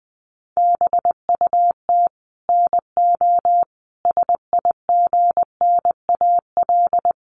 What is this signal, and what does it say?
Morse code, heard as a keyed tone, about 20 wpm, 700 Hz. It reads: BUT NO SIGNAL
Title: Sentences from Top 1400 Words 20wpm